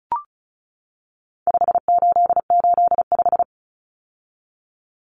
Morse code, heard as a keyed tone, about 34 wpm, 700 Hz. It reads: 5885
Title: Numbers - 4-Digit 35wpm